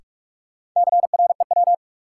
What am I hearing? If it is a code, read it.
Morse code, heard as a keyed tone, about 46 wpm, 700 Hz. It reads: CREW
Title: Sentences from Top 1500 Words - New Word Drill - 45wpm